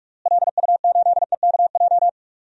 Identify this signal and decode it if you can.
Morse code, heard as a keyed tone, about 44 wpm, 700 Hz. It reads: LU8EKJ
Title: DXCC-100 Call Signs - 6 Characters - Repeated 3x - 45wpm